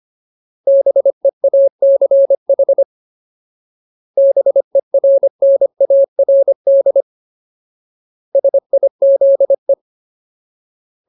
Morse code, heard as a keyed tone, about 25 wpm, 550 Hz. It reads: BEACH BERNARD SIZE
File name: Ultimate Word List / 25WPM Ultimate